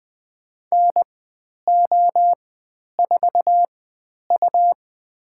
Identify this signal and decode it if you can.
Morse code, heard as a keyed tone, about 20 wpm, spaced as for 10 wpm, 700 Hz. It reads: NO4U
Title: 102 Lesson 17g - Letter u - Call Signs 20wpm